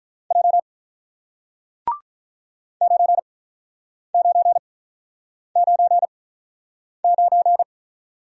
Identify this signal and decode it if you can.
Morse code, heard as a keyed tone, about 47 wpm, spaced as for 9 wpm, 700 Hz. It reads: J 9999